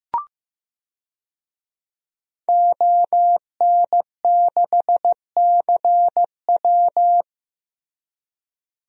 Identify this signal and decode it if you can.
Morse code, heard as a keyed tone, about 15 wpm, 700 Hz. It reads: ON6CW
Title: DXCC-100 Call Signs - 5 Characters - Repeated 3x - 15wpm